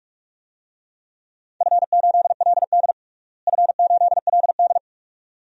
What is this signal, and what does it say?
Morse code, heard as a keyed tone, about 44 wpm, 700 Hz. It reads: F8LD F8LD
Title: DXCC-100 Call Signs - 4 Characters - Repeated 2x - 45wpm